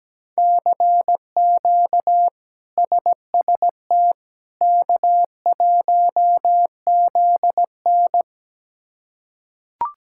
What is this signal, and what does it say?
Morse code, heard as a keyed tone, about 17 wpm, 700 Hz. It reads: CQ SST K1ZN
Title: K1USN SST QSO Practice - 17wpm